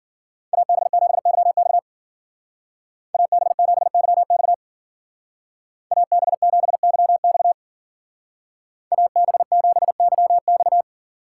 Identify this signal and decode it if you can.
Morse code, heard as a keyed tone, about 48 wpm, 700 Hz. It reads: AB7YX AB7YX AB7YX AB7YX
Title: Speed Racing - USA 2x2 Call Signs 40wpm